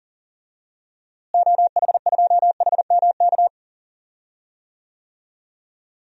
Morse code, heard as a keyed tone, about 38 wpm, 700 Hz. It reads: OH2HMK
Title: DXCC-100 Call Signs - 6 Characters - Repeated 3x - 40wpm